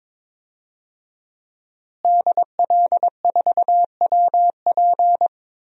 Morse code, heard as a keyed tone, about 21 wpm, 700 Hz. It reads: DL4WP
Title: DXCC-100 Call Signs - 5 Characters - Repeated 2x - 22wpm